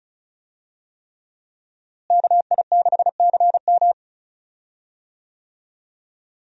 Morse code, heard as a keyed tone, about 36 wpm, 700 Hz. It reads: KI6CM
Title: USA Call Signs - 5 Characters - Repeated 3x - 35wpm